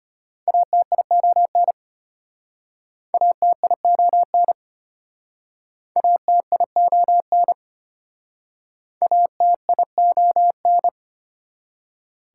Morse code, heard as a keyed tone, about 33 wpm, 700 Hz. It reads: ATION ATION ATION ATION